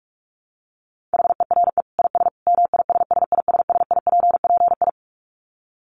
Morse code, heard as a keyed tone, about 44 wpm, 700 Hz. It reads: HERE IS MISSISSIPPI
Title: Dit Challenge 45wpm